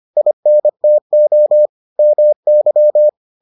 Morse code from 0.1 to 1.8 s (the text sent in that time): INTO